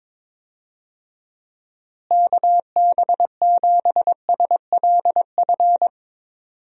Morse code, heard as a keyed tone, about 22 wpm, 700 Hz. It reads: KB7SLF
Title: USA Call Signs - 6 Characters - Repeated 2x - 22wpm